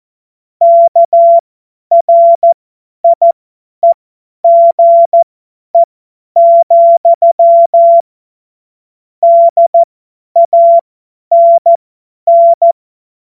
Morse code, heard as a keyed tone, about 14 wpm, spaced as for 10 wpm, 680 Hz. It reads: KRIEGE, DANN
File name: Morsekurs / Witze / Witz 02. November 2021, 50 F70